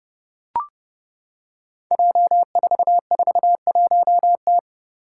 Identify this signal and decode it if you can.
Morse code, heard as a keyed tone, about 32 wpm, 700 Hz. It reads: J441T
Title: Five Challenging Letter-Number Combinations 30wpm